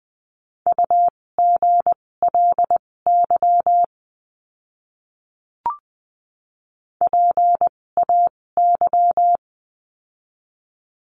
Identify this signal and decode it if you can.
Morse code, heard as a keyed tone, about 20 wpm, spaced as for 16 wpm, 700 Hz. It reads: UGLY PAY
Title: 256 Lesson 42a - Words 12-20wpm